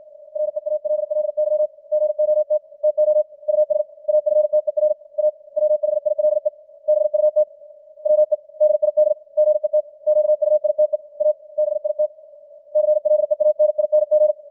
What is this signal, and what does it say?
Morse code, heard as a keyed tone, about 56 wpm, 610 Hz. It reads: REALLY GOT TO US AFTER A WHILE BUT WE DID GET QUITE A BIT CLEANING
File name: archived / CWBros